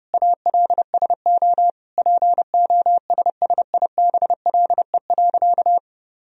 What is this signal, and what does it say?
Morse code, heard as a keyed tone, about 32 wpm, 700 Hz. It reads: ALSO POSSIBLE.
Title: Sentences from Top 1000 Words - Review Part 1 - 30wpm